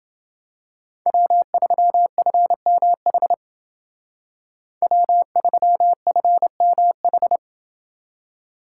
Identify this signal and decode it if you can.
Morse code, heard as a keyed tone, about 28 wpm, 700 Hz. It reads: W3FMH W3FMH